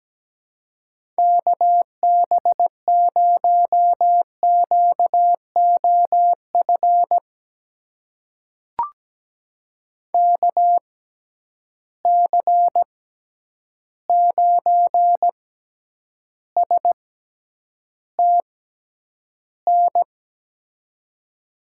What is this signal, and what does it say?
Morse code, heard as a keyed tone, about 17 wpm, spaced as for 9 wpm, 700 Hz. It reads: KB0QOF KC9STN
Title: USA Call Signs - 6x Character Spacing - 17wpm